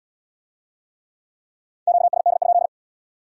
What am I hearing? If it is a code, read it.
Morse code, heard as a keyed tone, about 66 wpm, 700 Hz. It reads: QSD?